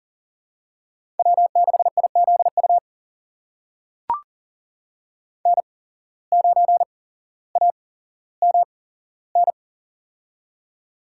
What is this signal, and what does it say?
Morse code, heard as a keyed tone, about 37 wpm, spaced as for 17 wpm, 700 Hz. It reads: W6IZU N9AMN
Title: USA Call Signs - 5 Characters - 8x Character Spacing - 40wpm